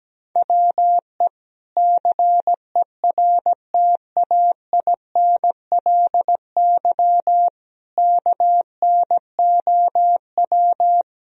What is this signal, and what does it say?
Morse code, heard as a keyed tone, about 17 wpm, 700 Hz. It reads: WE CERTAINLY KNOW
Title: Sentences from Top 1000 Words - Review Part 4 - 17wpm